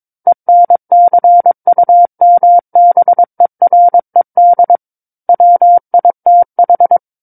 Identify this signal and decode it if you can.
Morse code, heard as a keyed tone, about 22 wpm, 700 Hz. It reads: ENCUMBERED WITH
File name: Book One / 13-Chapter 13 - 22 WPM